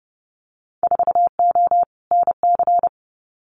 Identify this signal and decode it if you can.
Morse code, heard as a keyed tone, about 30 wpm, 700 Hz. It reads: 4O NC
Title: Winter Field Day Exchanges 30wpm